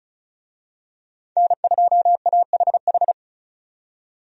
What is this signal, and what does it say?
Morse code, heard as a keyed tone, about 36 wpm, 700 Hz. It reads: N2AHH
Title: USA Call Signs - 5 Characters - 35wpm